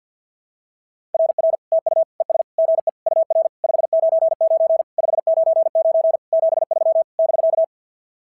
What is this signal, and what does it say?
Morse code, heard as a keyed tone, about 47 wpm, 690 Hz. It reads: RR TU ES GE UR 599 599 73 <BK>
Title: SOTA QSO Practice 50wpm